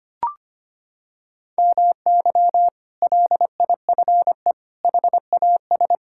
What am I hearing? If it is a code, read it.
Morse code, heard as a keyed tone, about 25 wpm, 700 Hz. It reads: MY LIFE HAS
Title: Sentences from Top 1000 Words - Review Part 6 - 25wpm